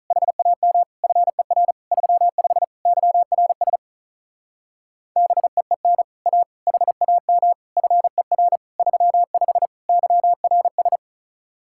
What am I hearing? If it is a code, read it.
Morse code, heard as a keyed tone, about 47 wpm, 700 Hz. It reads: HAM FER 35 YRS BEEN A HAM FER 35 YRS